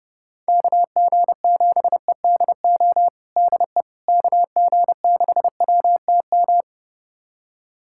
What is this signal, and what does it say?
Morse code, heard as a keyed tone, about 30 wpm, 700 Hz. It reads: KG7EDO DE KG6WTM
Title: USA Callsign DE Callsign 30wpm